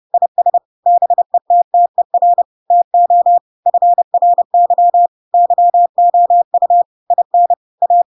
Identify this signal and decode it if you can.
Morse code, heard as a keyed tone, about 30 wpm, 700 Hz. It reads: IS BETTER TO FRY YOU IN A
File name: Pinocchio / Pinocchio ch5 30 30 ik1hge